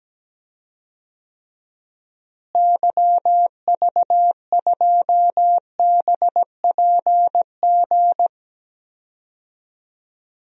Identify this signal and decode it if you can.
Morse code, heard as a keyed tone, about 17 wpm, 700 Hz. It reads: YV2BPG
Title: DXCC-100 Call Signs - 6 Characters - Repeated 2x - 17wpm